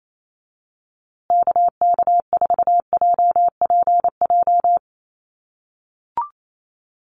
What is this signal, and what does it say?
Morse code, heard as a keyed tone, about 28 wpm, 700 Hz. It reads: KK4JPJ
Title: USA Call Signs Repeated 2x 28wpm